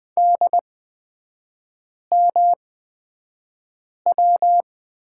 Morse code, heard as a keyed tone, about 20 wpm, spaced as for 5 wpm, 700 Hz. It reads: DMW
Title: 243 Lesson 39e - Letter z - Three Characters 20wpm